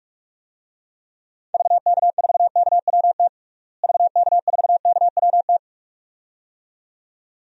Morse code, heard as a keyed tone, about 48 wpm, 700 Hz. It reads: VK4KWT VK4KWT
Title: DXCC-100 Call Signs - 6 Characters - Repeated 2x - 45wpm